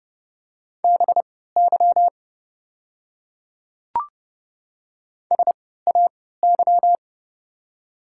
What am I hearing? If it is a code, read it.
Morse code, heard as a keyed tone, about 31 wpm, spaced as for 17 wpm, 690 Hz. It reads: BY SAY
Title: 3x Character Spacing - Top 300 Words - 30wpm